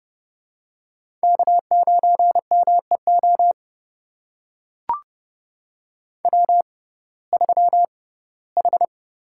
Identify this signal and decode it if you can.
Morse code, heard as a keyed tone, about 30 wpm, spaced as for 18 wpm, 700 Hz. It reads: K9MEO W3H
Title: USA Call Signs - 5 Characters - 6x Character Spacing - 30wpm